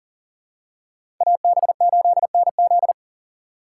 Morse code, heard as a keyed tone, about 40 wpm, 700 Hz. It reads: AB8NZ
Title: USA Call Signs - 5 Characters - 2x Character Spacing - 40wpm